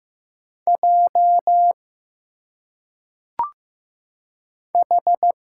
Morse code, heard as a keyed tone, about 15 wpm, 700 Hz. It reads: J H
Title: Single Letters - Phonetic Pronunciation 15wpm